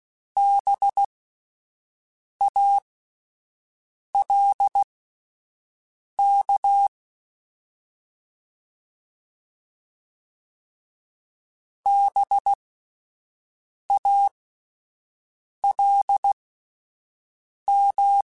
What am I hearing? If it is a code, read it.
Morse code, heard as a keyed tone, about 16 wpm, spaced as for 5 wpm, 800 Hz. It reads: BALK BALM
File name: Spaced Alphas / Spaced Alphas mp3s / Bs sp @10